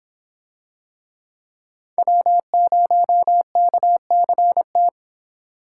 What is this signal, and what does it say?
Morse code, heard as a keyed tone, about 23 wpm, 700 Hz. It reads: W0KCT